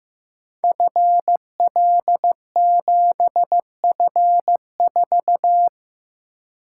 Morse code, heard as a keyed tone, about 15 wpm, 700 Hz. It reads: FL7F4
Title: Code Groups - 5 Letter:Numbers - 15wpm